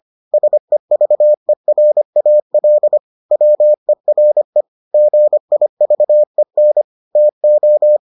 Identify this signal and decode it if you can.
Morse code, heard as a keyed tone, about 25 wpm, 600 Hz. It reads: SEVERAL WERE GIVEN TO